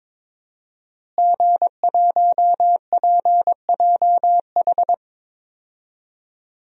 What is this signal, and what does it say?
Morse code, heard as a keyed tone, about 22 wpm, 700 Hz. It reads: G1PJH
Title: DXCC-100 Call Signs - 5 Characters - Repeated 3x - 22wpm